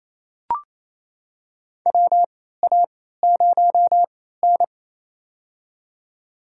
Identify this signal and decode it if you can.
Morse code, heard as a keyed tone, about 28 wpm, spaced as for 16 wpm, 700 Hz. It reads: WA0N
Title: USA Call Signs - 4 Characters - 3x Character Spacing - 28wpm